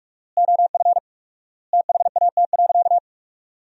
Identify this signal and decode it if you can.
Morse code, heard as a keyed tone, about 41 wpm, 700 Hz. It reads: OF THAT.
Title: 4x Word-Spacing - Sentences from Top 300 Words - 45wpm